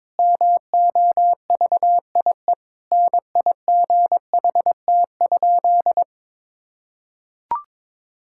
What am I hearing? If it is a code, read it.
Morse code, heard as a keyed tone, about 22 wpm, 700 Hz. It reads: MOVIE NIGHT?
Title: Sentences from Top 1000 Words - Review Part 5 - 22wpm